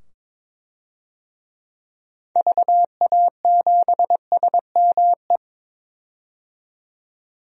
Morse code, heard as a keyed tone, about 22 wpm, 700 Hz. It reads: VA7SME
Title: DXCC-100 Call Signs - 6 Characters - 22wpm